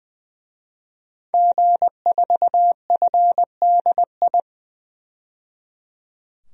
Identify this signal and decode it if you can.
Morse code, heard as a keyed tone, about 20 wpm, 700 Hz. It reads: G4FDI
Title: DXCC-100 Call Signs - 5 Characters - 20wpm